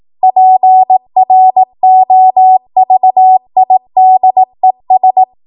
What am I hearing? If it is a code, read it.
Morse code, heard as a keyed tone, about 18 wpm, 750 Hz. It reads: PROVIDES
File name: Morse / 18WPM